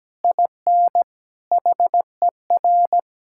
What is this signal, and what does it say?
Morse code, heard as a keyed tone, about 17 wpm, 700 Hz. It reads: IN HER
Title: Sentences from Top 1000 Words - Review Part 6 - 17wpm